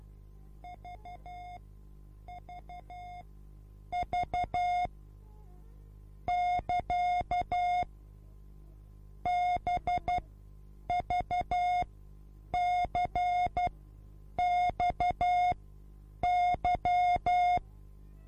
Morse code, heard as a keyed tone, about 12 wpm, spaced as for 8 wpm, 720 Hz. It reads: VVV <KA> BVCXY
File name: repos / Morsekurs / DJ4UF Morsen24